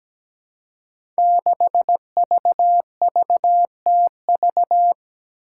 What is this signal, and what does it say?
Morse code, heard as a keyed tone, about 17 wpm, 700 Hz. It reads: 6VVTV
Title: Five Challenging Letter-Number Combinations 17wpm